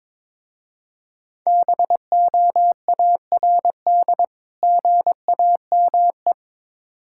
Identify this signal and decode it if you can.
Morse code, heard as a keyed tone, about 22 wpm, 700 Hz. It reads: BOARD GAME
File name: Difficult Words with Hints - 22wpm